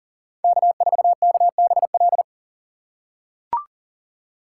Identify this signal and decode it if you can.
Morse code, heard as a keyed tone, about 40 wpm, 700 Hz. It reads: K4KBL
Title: K1USN SST QSO Practice - 40wpm